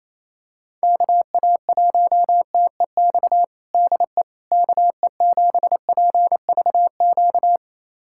Morse code, heard as a keyed tone, about 28 wpm, 700 Hz. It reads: KA1TEX DE KE7PVQ
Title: USA Callsign DE Callsign 28wpm